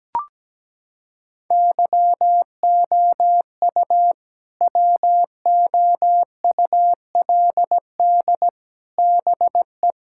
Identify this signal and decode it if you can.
Morse code, heard as a keyed tone, about 17 wpm, 700 Hz. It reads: YOU WOULD BE
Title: Sentences from Top 1000 Words - Review Part 6 - 17wpm